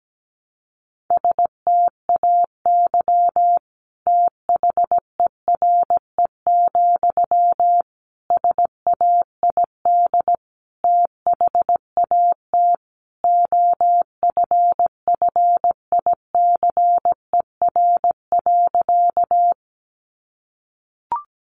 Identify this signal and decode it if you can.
Morse code, heard as a keyed tone, about 17 wpm, 700 Hz. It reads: STAY THERE, SAID THAT OFFICER.
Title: Sentences from Top 1200 Words 17wpm